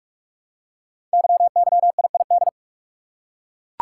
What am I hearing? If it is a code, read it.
Morse code, heard as a keyed tone, about 44 wpm, 700 Hz. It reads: YYIID
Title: Code Groups - 5 Letter:Numbers - 45wpm